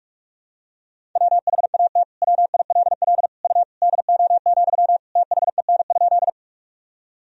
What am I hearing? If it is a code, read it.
Morse code, heard as a keyed tone, about 46 wpm, 700 Hz. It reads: WHAT WILL U DO, THEN?
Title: Sentences from Top 100 Words and Abbreviations 45wpm